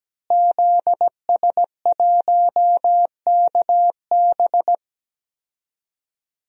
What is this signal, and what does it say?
Morse code, heard as a keyed tone, about 17 wpm, 700 Hz. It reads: ZS1KB
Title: DXCC-100 Call Signs - 5 Characters - Repeated 2x - 17wpm